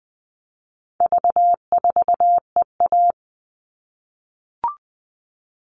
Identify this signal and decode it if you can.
Morse code, heard as a keyed tone, about 20 wpm, 700 Hz. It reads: V4EA
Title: All Cut Numbers - 4-Digit 20wpm